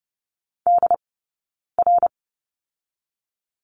Morse code, heard as a keyed tone, about 30 wpm, 700 Hz. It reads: D R
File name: Lesson 18d - Letter m - Two Characters 30wpm